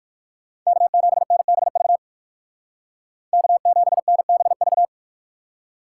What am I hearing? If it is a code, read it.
Morse code, heard as a keyed tone, about 58 wpm, 690 Hz. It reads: K7NBV K7NBV